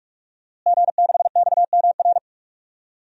Morse code, heard as a keyed tone, about 41 wpm, 700 Hz. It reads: G6XMR
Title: DXCC-100 Call Signs - 5 Characters - Repeated 2x - 45wpm